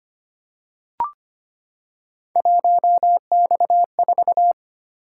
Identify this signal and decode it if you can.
Morse code, heard as a keyed tone, about 25 wpm, 700 Hz. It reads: 1X4
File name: Three Letter-Number Combinations - 25wpm